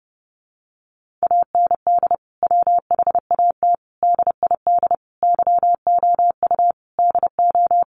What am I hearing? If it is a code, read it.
Morse code, heard as a keyed tone, about 28 wpm, 700 Hz. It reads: AND WHAT DID YOU DO
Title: Sentences from Top 100 Words 30wpm